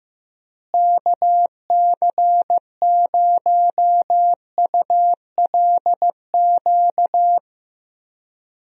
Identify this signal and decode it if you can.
Morse code, heard as a keyed tone, about 15 wpm, 700 Hz. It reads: KC0ULQ
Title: USA Call Signs - 6 Characters - Repeated 3x - 15wpm